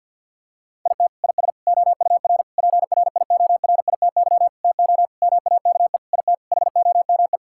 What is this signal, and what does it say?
Morse code, heard as a keyed tone, about 46 wpm, 700 Hz. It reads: IT IS OUR PRIORITY TO MAKE IT HOME
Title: Sentences from Top 1600 Words 50wpm